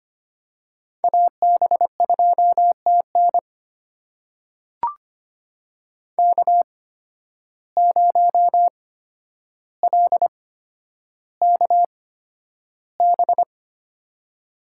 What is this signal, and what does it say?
Morse code, heard as a keyed tone, about 25 wpm, spaced as for 11 wpm, 700 Hz. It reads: AB2TN K0LKB
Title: USA Call Signs - 8x Character Spacing - 25wpm